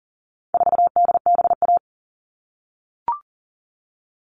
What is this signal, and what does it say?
Morse code, heard as a keyed tone, about 40 wpm, 700 Hz. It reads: 4DBA
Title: All Cut Numbers - 4-Digit 40wpm